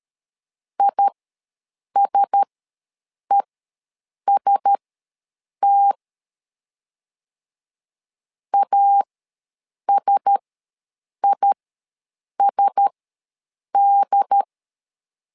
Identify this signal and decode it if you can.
Morse code, heard as a keyed tone, about 13 wpm, spaced as for 7 wpm, 800 Hz. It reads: ISEST ASISD
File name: LEZIONI / 4 Q1S ws 12wpm 1827